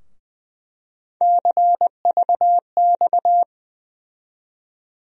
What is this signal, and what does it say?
Morse code, heard as a keyed tone, about 20 wpm, 700 Hz. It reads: CVX
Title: USA Call Sign Postfixes - 3 Letters 20wpm